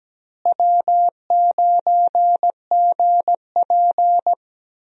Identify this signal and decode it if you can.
Morse code, heard as a keyed tone, about 17 wpm, 700 Hz. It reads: W9GP
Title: USA Call Signs - 4 Characters - Repeated 2x - 17wpm